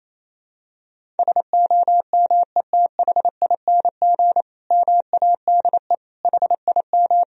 Morse code, heard as a keyed tone, about 28 wpm, 700 Hz. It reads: SOMETHING MADE HIM
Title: Sentences from Top 1000 Words - Review Part 1 - 28wpm